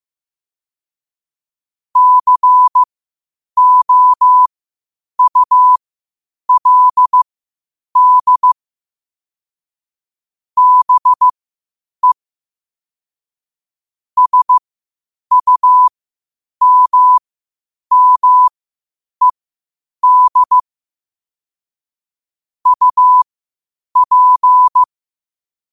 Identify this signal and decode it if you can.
Morse code, heard as a keyed tone, about 15 wpm, spaced as for 8 wpm, 1000 Hz. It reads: COULD BE SUMMED UP
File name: qotdfiles / 8wpm